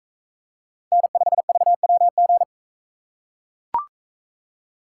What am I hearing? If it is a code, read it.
Morse code, heard as a keyed tone, about 41 wpm, 700 Hz. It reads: N5VWG